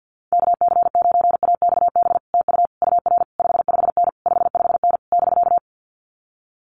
Morse code, heard as a keyed tone, about 50 wpm, 700 Hz. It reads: KC9AXB TU UR 55N 55N <BK>
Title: Common QSO Phrases Encore 50wpm